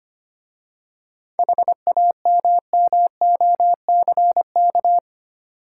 Morse code, heard as a keyed tone, about 28 wpm, 700 Hz. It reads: HAMMOCK
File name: Easy Words - Speed-Racing with Hints - 25wpm